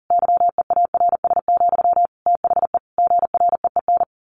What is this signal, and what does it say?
Morse code, heard as a keyed tone, about 40 wpm, 700 Hz. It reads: YEARS, THE GREEN
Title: A Sound Of Thunder 40wpm